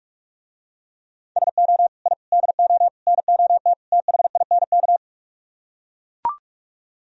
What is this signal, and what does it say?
Morse code, heard as a keyed tone, about 44 wpm, 700 Hz. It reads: SO I DO NOT THINK